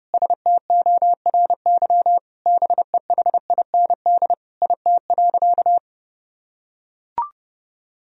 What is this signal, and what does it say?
Morse code, heard as a keyed tone, about 30 wpm, 700 Hz. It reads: STORY BEHIND IT.
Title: Sentences from Top 500 Words and Abbreviations 30wpm